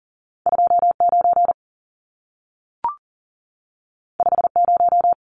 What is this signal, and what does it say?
Morse code, heard as a keyed tone, about 40 wpm, 700 Hz. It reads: 29 50
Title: Numbers - 2-Digit 40wpm